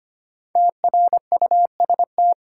Morse code, heard as a keyed tone, about 25 wpm, 700 Hz. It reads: TRUST
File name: 5 Letter Words - First 3 Letters Given - 25wpm